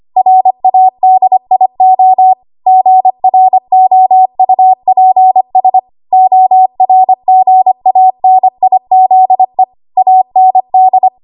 Morse code, heard as a keyed tone, about 25 wpm, 750 Hz. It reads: RADIO GROUPS ORGANIZE AND
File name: Morse / 25WPM